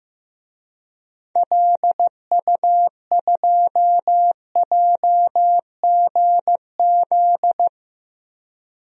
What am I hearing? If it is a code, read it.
Morse code, heard as a keyed tone, about 15 wpm, 700 Hz. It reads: LU2JGZ
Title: DXCC-100 Call Signs - 6 Characters - 15wpm